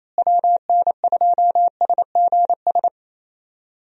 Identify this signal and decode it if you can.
Morse code, heard as a keyed tone, about 28 wpm, 700 Hz. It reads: WN2SGS
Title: USA Call Signs - 6 Characters - Repeated 3x - 28wpm